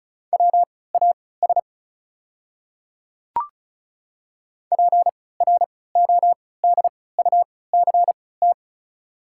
Morse code, heard as a keyed tone, about 34 wpm, spaced as for 20 wpm, 690 Hz. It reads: WAS PRODUCT
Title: 3x Character Spacing - Top 500 Words - 35wpm